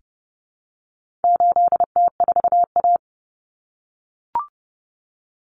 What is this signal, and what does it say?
Morse code, heard as a keyed tone, about 30 wpm, 700 Hz. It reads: 8T4A
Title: Common Cut Numbers - 4-Digit 30wpm